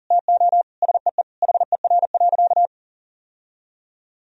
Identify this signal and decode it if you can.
Morse code, heard as a keyed tone, about 37 wpm, 700 Hz. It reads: TO SEE HER.
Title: Sentences from Top 1000 Words - Review Part 2 - 40wpm